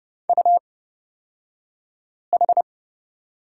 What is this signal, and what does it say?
Morse code, heard as a keyed tone, about 30 wpm, spaced as for 5 wpm, 700 Hz. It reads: UH
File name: Lesson 38d - Letter q - Two Characters 30wpm